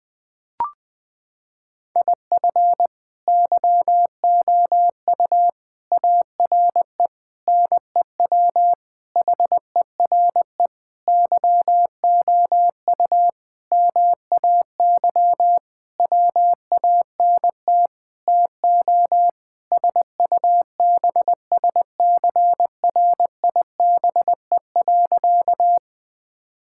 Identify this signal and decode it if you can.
Morse code, heard as a keyed tone, about 20 wpm, 700 Hz. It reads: IF YOU ARE NEW HERE YOU MAY WANT TO SUBSCRIBE.
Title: Sentences with 8-Word Phrases - 20wpm